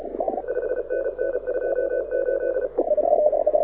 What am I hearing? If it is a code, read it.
Morse code, heard as a keyed tone, about 33 wpm, 1500 Hz, with radio noise. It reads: 5NN 28
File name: audio zawody / IARU 2015 CW SN7Q